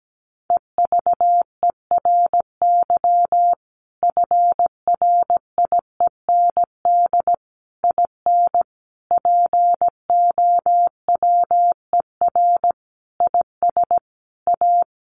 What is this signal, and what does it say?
Morse code, heard as a keyed tone, about 17 wpm, 700 Hz. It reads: EVERY FRIEND IN POWER IS A
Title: Sentences from Top 1400 Words 17wpm